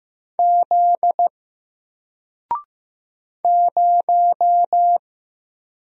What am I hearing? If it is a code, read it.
Morse code, heard as a keyed tone, about 15 wpm, 700 Hz. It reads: Z 0
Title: Single Letter-Number 15wpm